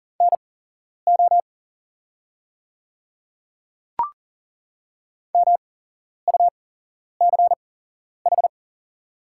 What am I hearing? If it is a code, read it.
Morse code, heard as a keyed tone, about 40 wpm, spaced as for 11 wpm, 690 Hz. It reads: NO MUCH
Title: 8x Character Spacing - Top 200 Words - 40wpm